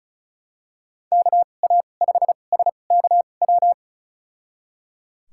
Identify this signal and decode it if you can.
Morse code, heard as a keyed tone, about 36 wpm, spaced as for 25 wpm, 700 Hz. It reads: KA5SKW
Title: USA Call Signs - 6 Characters - 2x Character Spacing - 35wpm